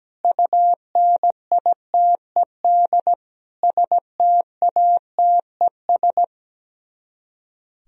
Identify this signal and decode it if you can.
Morse code, heard as a keyed tone, about 17 wpm, 700 Hz. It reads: UNITED STATES
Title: Speed-Racing - Sets of 4 Words 17wpm